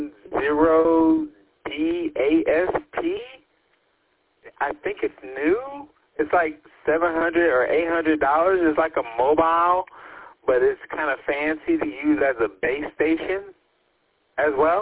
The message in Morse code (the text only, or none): none